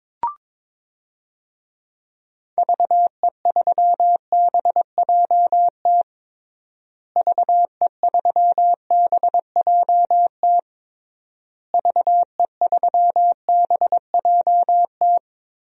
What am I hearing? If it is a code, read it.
Morse code, heard as a keyed tone, about 22 wpm, 700 Hz. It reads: VE3BJT VE3BJT VE3BJT
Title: DXCC-100 Call Signs - 6 Characters - Repeated 3x - 22wpm